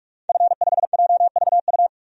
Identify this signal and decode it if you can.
Morse code, heard as a keyed tone, about 46 wpm, 700 Hz. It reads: F5JVU
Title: DXCC-100 Call Signs - 5 Characters - 45wpm